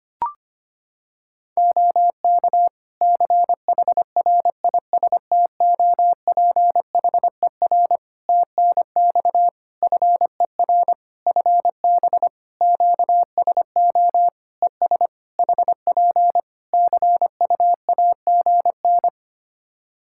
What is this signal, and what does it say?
Morse code, heard as a keyed tone, about 25 wpm, 700 Hz. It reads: OK CHRISTOPHER TNX FER FB QSO ES HP CUAGN
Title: Practice QSO - Exchange 4 - Ending - 25wpm